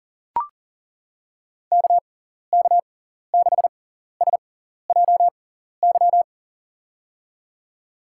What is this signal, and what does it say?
Morse code, heard as a keyed tone, about 41 wpm, spaced as for 14 wpm, 700 Hz. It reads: KK6SJY
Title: USA Call Signs - 6 Characters - 6x Character Spacing - 40wpm